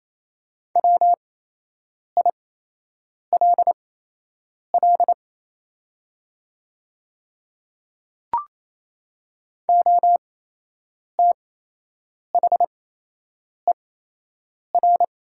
Morse code, heard as a keyed tone, about 28 wpm, spaced as for 8 wpm, 700 Hz. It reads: WILL OTHER
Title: 8x Character Spacing - Top 200 Words - 28wpm